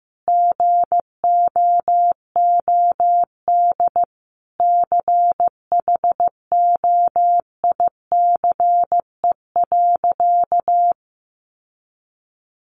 Morse code, heard as a keyed tone, about 15 wpm, 700 Hz. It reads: GOOD CHOICE.
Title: Sentences from Top 900 Words 15wpm